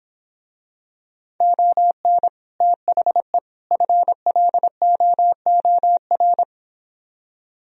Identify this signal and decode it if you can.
Morse code, heard as a keyed tone, about 26 wpm, 700 Hz. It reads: ON THE FLOOR